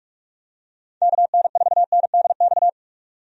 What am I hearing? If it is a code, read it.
Morse code, heard as a keyed tone, about 46 wpm, 700 Hz. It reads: KN4NDX
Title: USA Call Signs - 6 Characters - 45wpm